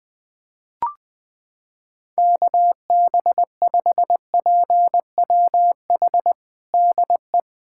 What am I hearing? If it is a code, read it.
Morse code, heard as a keyed tone, about 20 wpm, 700 Hz. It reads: KB5PWH DE
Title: USA Callsign DE Callsign 20wpm